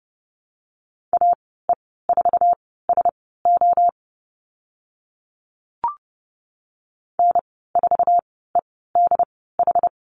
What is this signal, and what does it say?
Morse code, heard as a keyed tone, about 30 wpm, spaced as for 17 wpm, 700 Hz. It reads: AE4SO N4EDH
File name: Lesson 15g - Number 5 - Call Signs 30wpm